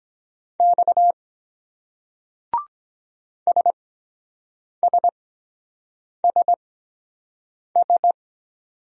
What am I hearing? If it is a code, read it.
Morse code, heard as a keyed tone, about 23 wpm, spaced as for 7 wpm, 700 Hz. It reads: X SSSS